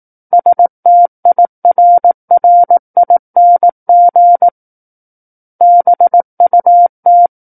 Morse code, heard as a keyed tone, about 18 wpm, 700 Hz. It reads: STIRRING BUT
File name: 18 WPM